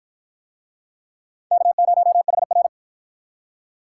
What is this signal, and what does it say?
Morse code, heard as a keyed tone, about 46 wpm, 700 Hz. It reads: K0HR